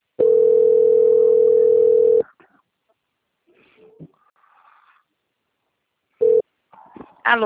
Ringback tone (North America), 440 + 480 Hz.